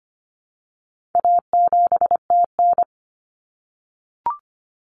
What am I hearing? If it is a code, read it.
Morse code, heard as a keyed tone, about 25 wpm, 700 Hz. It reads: A7TN
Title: Common Cut Numbers - 4-Digit 25wpm